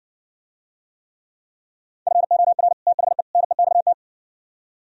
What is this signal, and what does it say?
Morse code, heard as a keyed tone, about 60 wpm, 700 Hz. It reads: FOR THE NEXT